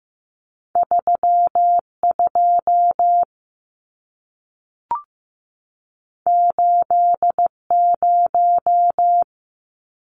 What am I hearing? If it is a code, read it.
Morse code, heard as a keyed tone, about 15 wpm, 700 Hz. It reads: 3 2 8 0
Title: Numbers - 2-Digit 15wpm